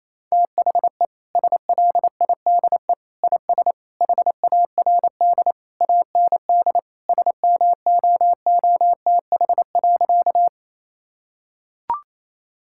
Morse code, heard as a keyed tone, about 28 wpm, 700 Hz. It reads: THE SLIDE IS HARD AND SMOOTH.
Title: Sentences from Top 1600 Words 28wpm